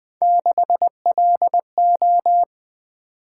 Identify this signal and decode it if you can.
Morse code, heard as a keyed tone, about 21 wpm, 700 Hz. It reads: 6LO
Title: Three Letter-Number Combinations - 20wpm